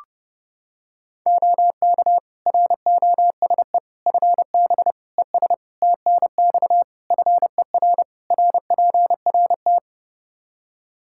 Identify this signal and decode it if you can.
Morse code, heard as a keyed tone, about 31 wpm, 700 Hz. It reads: OK ROSE FB ES TNX FER RPRT
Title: Practice QSO - Exchange 2 - Info - 30wpm